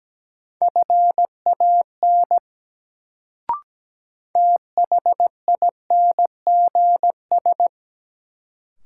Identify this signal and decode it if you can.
Morse code, heard as a keyed tone, about 17 wpm, 700 Hz. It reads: FAN THINGS
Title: Sentences from Top 1000 Words - Review Part 4 - New Word Drill - 17wpm